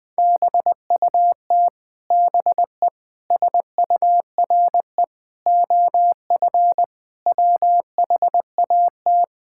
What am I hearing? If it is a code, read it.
Morse code, heard as a keyed tone, about 21 wpm, 700 Hz. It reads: BUT BE SURE OF WHAT
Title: Sentences from Top 1600 Words 20wpm